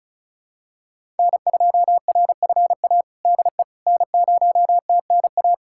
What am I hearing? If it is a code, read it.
Morse code, heard as a keyed tone, about 36 wpm, 700 Hz. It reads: N2RFA DE N0TNA
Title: USA Callsign DE Callsign - 5 Characters - 35wpm